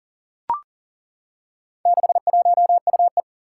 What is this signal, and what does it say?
Morse code, heard as a keyed tone, about 40 wpm, 700 Hz. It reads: 61UE
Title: Four Challenging Letter-Number Combinations 40wpm